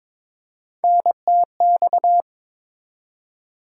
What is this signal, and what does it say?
Morse code, heard as a keyed tone, about 22 wpm, 700 Hz. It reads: NTX
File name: ARRL Sections v2 - 22wpm